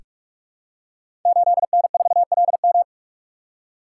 Morse code, heard as a keyed tone, about 44 wpm, 700 Hz. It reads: 8N4LM
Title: Code Groups - 5 Letter:Numbers - 45wpm